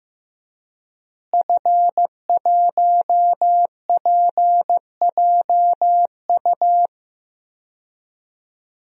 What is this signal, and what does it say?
Morse code, heard as a keyed tone, about 15 wpm, 700 Hz. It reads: F1PJU
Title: DXCC-100 Call Signs - 5 Characters - Repeated 2x - 15wpm